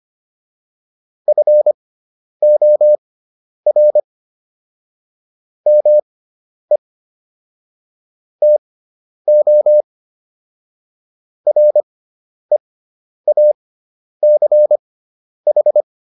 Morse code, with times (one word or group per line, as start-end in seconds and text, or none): 1.3-4.0 FOR
5.7-6.8 ME
8.4-9.8 TO
11.5-15.8 REACH